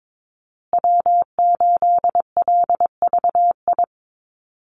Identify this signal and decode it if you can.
Morse code, heard as a keyed tone, about 22 wpm, 700 Hz. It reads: W8LVI
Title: Call Signs Repeated 2x - 22wpm